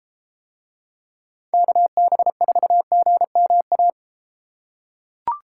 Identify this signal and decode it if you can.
Morse code, heard as a keyed tone, about 31 wpm, 700 Hz. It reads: KB4GMA